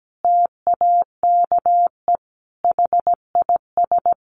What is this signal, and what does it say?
Morse code, heard as a keyed tone, about 17 wpm, 700 Hz. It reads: TAKE HIS
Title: Sentences from Top 900 Words 17wpm